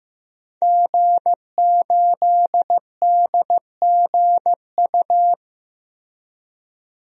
Morse code, heard as a keyed tone, about 15 wpm, 700 Hz. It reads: G8DGU
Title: DXCC-100 Call Signs - 5 Characters - Repeated 3x - 15wpm